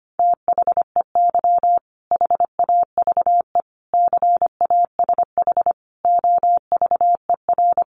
Morse code, heard as a keyed tone, about 25 wpm, 700 Hz. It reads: THEY HAVE CASH OVER
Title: Sentences from Top 700 Words 25wpm